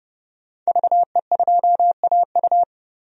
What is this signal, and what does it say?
Morse code, heard as a keyed tone, about 30 wpm, 700 Hz. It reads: VE2AU
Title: DXCC-100 Call Signs - 5 Characters - 30wpm